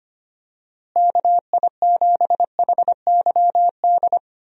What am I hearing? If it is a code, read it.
Morse code, heard as a keyed tone, about 25 wpm, 700 Hz. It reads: KI7HYD
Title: USA Call Signs - 6 Characters - Repeated 3x - 25wpm